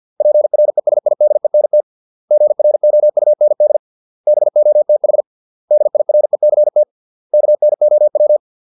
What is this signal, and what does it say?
Morse code, heard as a keyed tone, about 50 wpm, 600 Hz. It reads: PRESIDENT GROUND BOTH DIRECT KNOW
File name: 632 words / 632 Words 50WPM